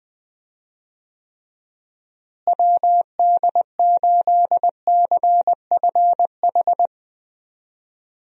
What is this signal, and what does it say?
Morse code, heard as a keyed tone, about 20 wpm, 700 Hz. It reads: WD8CFH
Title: USA Call Signs - 6 Characters - Repeated 2x - 20wpm